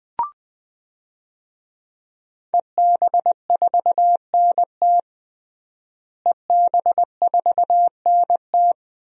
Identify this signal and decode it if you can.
Morse code, heard as a keyed tone, about 21 wpm, 700 Hz. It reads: EB4NT EB4NT
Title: DXCC-100 Call Signs - 5 Characters - Repeated 2x - 20wpm